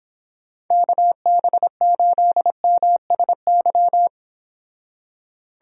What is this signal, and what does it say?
Morse code, heard as a keyed tone, about 29 wpm, 700 Hz. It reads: KB8MSY